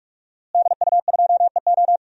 Morse code, heard as a keyed tone, about 46 wpm, 700 Hz. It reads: DU2EO
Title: DXCC-100 Call Signs - 5 Characters - Repeated 3x - 45wpm